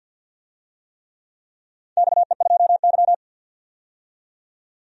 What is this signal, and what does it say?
Morse code, heard as a keyed tone, about 46 wpm, 700 Hz. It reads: XE2Y
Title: DXCC-100 Call Signs - 4 Characters - 50wpm